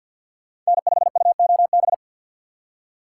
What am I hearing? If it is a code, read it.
Morse code, heard as a keyed tone, about 60 wpm, 700 Hz. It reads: N5UOB
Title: Speed Racing - USA 1x3 Call Signs 50wpm